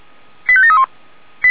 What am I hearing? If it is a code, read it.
Five-tone selective call (ZVEI), heard as quick runs of five tones, tones of about 70 ms. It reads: 85731